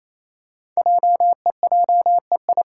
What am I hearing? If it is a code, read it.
Morse code, heard as a keyed tone, about 28 wpm, 700 Hz. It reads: JEJEI
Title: Code Groups - 5 Letter:Numbers - 28wpm